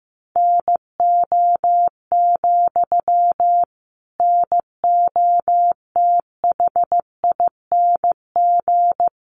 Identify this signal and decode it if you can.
Morse code, heard as a keyed tone, about 15 wpm, 700 Hz. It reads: NO, NOTHING
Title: Sentences from Top 800 Words 15wpm